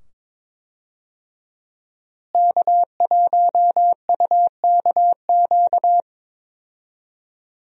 Morse code, heard as a keyed tone, about 22 wpm, 700 Hz. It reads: K1UKQ
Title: USA Call Signs - 5 Characters - Repeated 2x - 22wpm